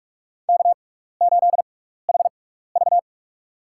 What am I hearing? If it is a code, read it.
Morse code, heard as a keyed tone, about 42 wpm, spaced as for 15 wpm, 700 Hz. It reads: K8HV
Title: USA Call Signs - 4 Characters - 6x Character Spacing - 45wpm